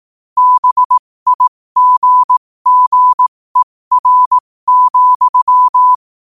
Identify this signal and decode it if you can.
Morse code, heard as a keyed tone, about 18 wpm, spaced as for 15 wpm, 1000 Hz. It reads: BIGGER,
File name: qotdfiles / 15wpm